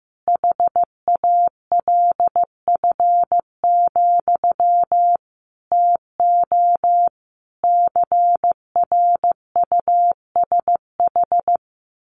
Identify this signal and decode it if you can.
Morse code, heard as a keyed tone, about 15 wpm, 700 Hz. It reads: HALF, TO CRUSH
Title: A Sound Of Thunder 15wpm